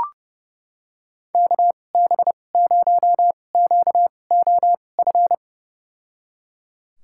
Morse code, heard as a keyed tone, about 30 wpm, spaced as for 22 wpm, 700 Hz. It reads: KB0QOF
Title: USA Call Signs - 6 Characters - 2x Character Spacing - 30wpm